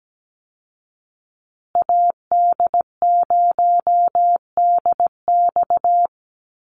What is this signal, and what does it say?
Morse code, heard as a keyed tone, about 17 wpm, 700 Hz. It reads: AD0DX
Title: Call Signs Repeated 2x - 17wpm